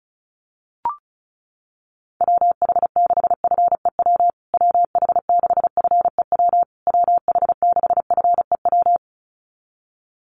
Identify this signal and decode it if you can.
Morse code, heard as a keyed tone, about 34 wpm, 700 Hz. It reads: WH6FEW WH6FEW WH6FEW
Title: USA Call Signs Repeated 3x 35wpm